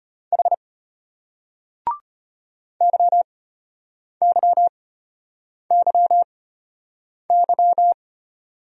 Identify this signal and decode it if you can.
Morse code, heard as a keyed tone, about 33 wpm, spaced as for 8 wpm, 700 Hz. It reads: H YYYY